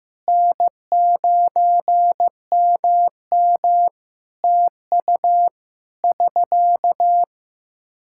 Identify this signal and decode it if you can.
Morse code, heard as a keyed tone, about 15 wpm, 700 Hz. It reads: N9MM TU <SK>
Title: Practice QSO - Exchange 4 - Ending - 15wpm